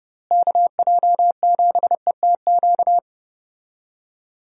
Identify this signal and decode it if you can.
Morse code, heard as a keyed tone, about 30 wpm, 700 Hz. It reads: KJ7ETQ
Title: USA Call Signs - 6 Characters - Repeated 2x - 30wpm